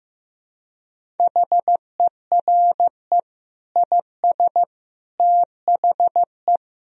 Morse code, heard as a keyed tone, about 15 wpm, 700 Hz. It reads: HERE IS THE
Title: Sentences from Top 1000 Words - Review Part 1 - 15wpm